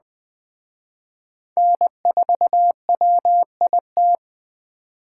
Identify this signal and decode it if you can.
Morse code, heard as a keyed tone, about 23 wpm, 700 Hz. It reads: N4WIT